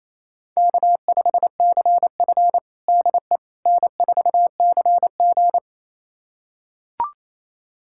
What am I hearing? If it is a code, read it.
Morse code, heard as a keyed tone, about 27 wpm, 700 Hz. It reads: K5CF DE N4CG
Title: USA Callsign DE Callsign - 4 Characters - 28wpm